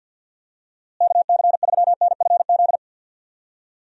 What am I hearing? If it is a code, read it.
Morse code, heard as a keyed tone, about 50 wpm, 700 Hz. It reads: KC3NFZ
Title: USA Call Signs - 6 Characters - Repeated 3x - 50wpm